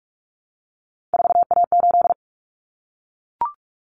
Morse code, heard as a keyed tone, about 44 wpm, 700 Hz. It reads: 4A8
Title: Common Cut Numbers - 3-Digit 45wpm